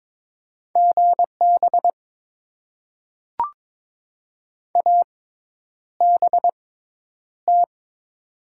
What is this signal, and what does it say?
Morse code, heard as a keyed tone, about 22 wpm, spaced as for 10 wpm, 700 Hz. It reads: GB ABT
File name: Top 90 QSO Elements - 6x Character Spacing - 22wpm